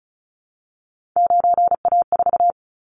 Morse code, heard as a keyed tone, about 36 wpm, 690 Hz. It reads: 9A4
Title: Callsign Prefixes for Top 50 DXCC - 35wpm